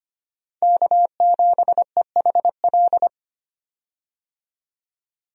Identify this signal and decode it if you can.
Morse code, heard as a keyed tone, about 25 wpm, 700 Hz. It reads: K7EHL
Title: USA Call Signs - 5 Characters - 25wpm